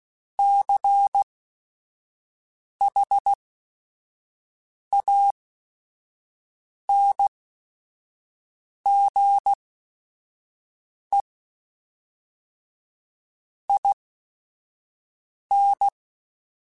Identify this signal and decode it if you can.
Morse code, heard as a keyed tone, about 16 wpm, spaced as for 5 wpm, 800 Hz. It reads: CHANGE IN
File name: Technical Subjects / Technical Subjects-mp3s / XL @5